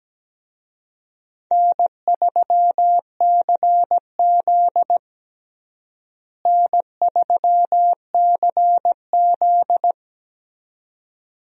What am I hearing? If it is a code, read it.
Morse code, heard as a keyed tone, about 17 wpm, 700 Hz. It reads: N3CZ N3CZ
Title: USA Call Signs - 4 Characters - Repeated 2x - 17wpm